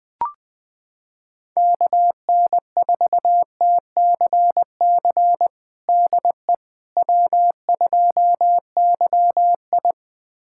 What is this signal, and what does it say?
Morse code, heard as a keyed tone, about 20 wpm, 700 Hz. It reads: KN4TCC DE W2YI
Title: USA Callsign DE Callsign 20wpm